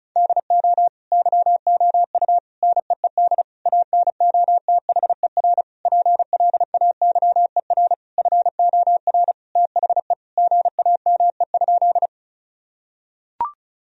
Morse code, heard as a keyed tone, about 34 wpm, 700 Hz. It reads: DO YOU NEED ANOTHER PLAYER FOR THE GAME?
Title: Sentences from Top 1000 Words - Review Part 4 - 35wpm